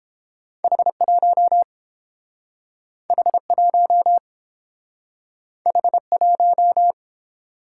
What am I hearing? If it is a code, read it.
Morse code, heard as a keyed tone, about 30 wpm, 700 Hz. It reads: H1 H1 H1